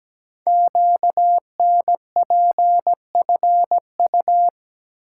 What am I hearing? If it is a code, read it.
Morse code, heard as a keyed tone, about 17 wpm, 700 Hz. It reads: QNPFU
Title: Code Groups - 5 Letters - 17wpm